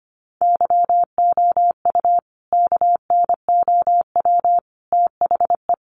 Morse code, heard as a keyed tone, about 25 wpm, 700 Hz. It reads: YOU KNOW THE
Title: Sentences from Top 1500 Words 25wpm